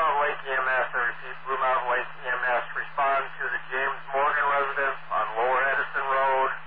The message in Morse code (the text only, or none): none